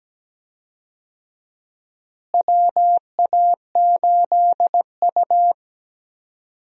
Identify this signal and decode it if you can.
Morse code, heard as a keyed tone, about 17 wpm, 700 Hz. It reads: WA8U
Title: USA Call Signs - 4 Characters - Repeated 2x - 17wpm